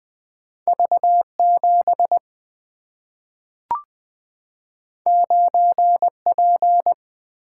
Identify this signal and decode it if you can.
Morse code, heard as a keyed tone, about 20 wpm, 700 Hz. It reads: V 7 9 P
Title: Single-Letter Single-Number 20wpm